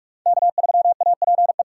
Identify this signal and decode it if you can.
Morse code, heard as a keyed tone, about 46 wpm, 690 Hz. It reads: K3APE
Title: USA Call Signs - 5 Characters - 6x Character Spacing - 45wpm